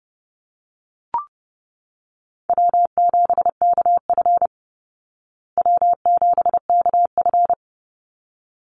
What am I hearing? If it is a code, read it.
Morse code, heard as a keyed tone, about 30 wpm, 700 Hz. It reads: W7KF W7KF
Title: Easy Call Signs Repeated 2x - 30wpm